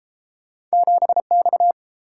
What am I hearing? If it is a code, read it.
Morse code, heard as a keyed tone, about 31 wpm, 700 Hz. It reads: 7X